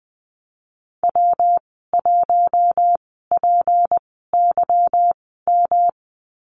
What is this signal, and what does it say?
Morse code, heard as a keyed tone, about 19 wpm, spaced as for 14 wpm, 700 Hz. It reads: W1PYM
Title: 154 Lesson 25g - Letter p - Call Signs 20wpm